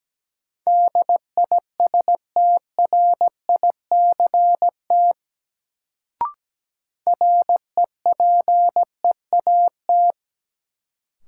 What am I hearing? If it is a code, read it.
Morse code, heard as a keyed tone, about 17 wpm, 700 Hz. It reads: DISTRICT REPEAT
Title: Sentences from Top 1600 Words - New Word Drill - 17wpm